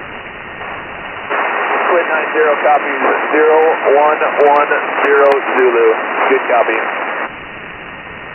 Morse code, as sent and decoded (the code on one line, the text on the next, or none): none